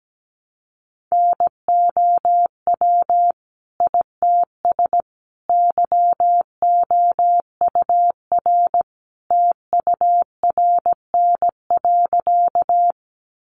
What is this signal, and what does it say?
Morse code, heard as a keyed tone, about 17 wpm, 700 Hz. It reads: NOW ITS YOUR TURN.
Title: Sentences from Top 600 Words 17wpm